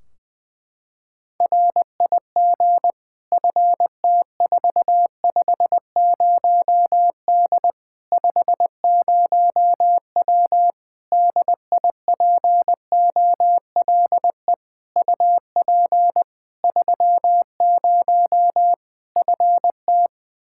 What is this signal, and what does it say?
Morse code, heard as a keyed tone, about 21 wpm, 700 Hz. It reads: RIG FT450D 50W DIPOLE UP 30 FT
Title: Practice QSO - Exchange 2 - Info - 20wpm